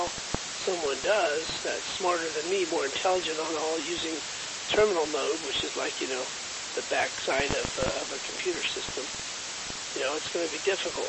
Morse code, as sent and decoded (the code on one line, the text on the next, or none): none